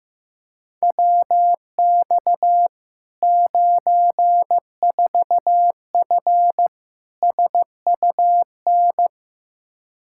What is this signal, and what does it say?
Morse code, heard as a keyed tone, about 15 wpm, 700 Hz. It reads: WX 94F SUN
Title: Practice QSO - Exchange 2 - Info - 15wpm